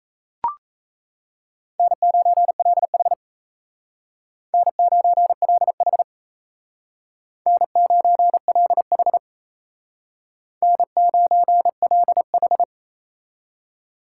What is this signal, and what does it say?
Morse code, heard as a keyed tone, about 36 wpm, 700 Hz. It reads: N9LH N9LH N9LH N9LH